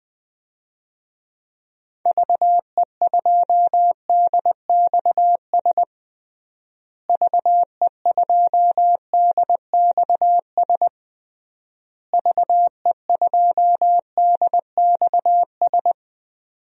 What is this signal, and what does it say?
Morse code, heard as a keyed tone, about 20 wpm, 700 Hz. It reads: VE2DXS VE2DXS VE2DXS
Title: DXCC-100 Call Signs - 6 Characters - Repeated 3x - 20wpm